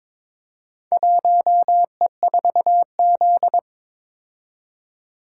Morse code, heard as a keyed tone, about 21 wpm, 700 Hz. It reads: 1E4Z
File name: practice / 22wpm / Four Letter-Number Combinations - 22wpm